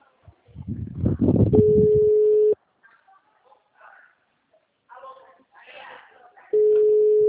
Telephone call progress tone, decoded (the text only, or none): ringback tone